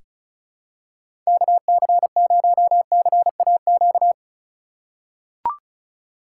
Morse code, heard as a keyed tone, about 34 wpm, 700 Hz. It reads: KC0CAQ
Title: USA Call Signs - 6 Characters - 3x Character Spacing - 35wpm